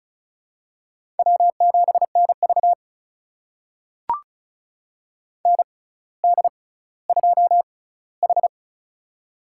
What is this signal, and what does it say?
Morse code, heard as a keyed tone, about 36 wpm, spaced as for 18 wpm, 700 Hz. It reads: W7NV ND2H
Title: USA Call Signs - 4 Characters - 6x Character Spacing - 35wpm